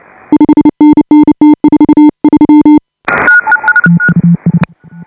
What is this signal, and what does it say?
Morse code, heard as a keyed tone, about 29 wpm, 310 Hz. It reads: 5NN T43
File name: AUDIO2018 / HA5BA CW